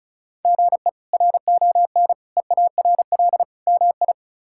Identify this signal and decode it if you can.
Morse code, heard as a keyed tone, about 36 wpm, 700 Hz. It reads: GE RON EARL MI
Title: K1USN SST QSO Practice - 35wpm